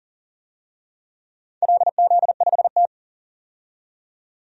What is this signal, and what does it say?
Morse code, heard as a keyed tone, about 40 wpm, 700 Hz. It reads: LZ5T
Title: DXCC-100 Call Signs - 4 Characters - 40wpm